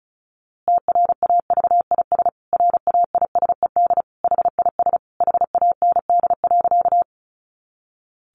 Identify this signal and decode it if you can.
Morse code, heard as a keyed tone, about 34 wpm, 700 Hz. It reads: TRAVIS RAISED HIS HAND.
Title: A Sound Of Thunder 35wpm